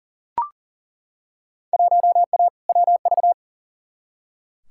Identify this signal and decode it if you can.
Morse code, heard as a keyed tone, about 40 wpm, 700 Hz. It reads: 1A WV
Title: Field Day Exchanges Encore - 40wpm